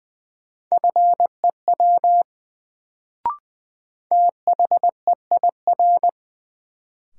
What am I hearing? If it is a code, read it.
Morse code, heard as a keyed tone, about 20 wpm, 700 Hz. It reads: FEW THEIR
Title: Sets of 2 Words - New Word Drill - 20wpm